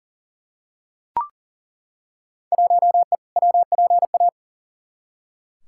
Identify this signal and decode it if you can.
Morse code, heard as a keyed tone, about 40 wpm, 700 Hz. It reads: 1E WPA
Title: Field Day Exchanges - 40wpm